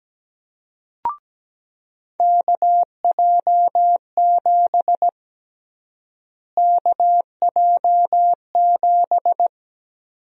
Morse code, heard as a keyed tone, about 17 wpm, 700 Hz. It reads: KJ7 KJ7
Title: USA Call Sign Prefixes - Repeated 2x - 17wpm